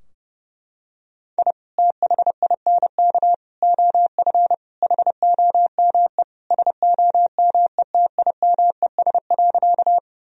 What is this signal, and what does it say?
Morse code, heard as a keyed tone, about 32 wpm, 700 Hz. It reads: I THINK OF HOME SOMETIMES.
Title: Sentences from Top 1000 Words - Review Part 3 - 30wpm